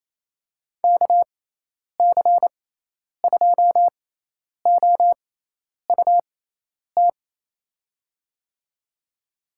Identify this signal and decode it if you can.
Morse code, heard as a keyed tone, about 28 wpm, spaced as for 10 wpm, 700 Hz. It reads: KC2OUT
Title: USA Call Signs - 6 Characters - 6x Character Spacing - 28wpm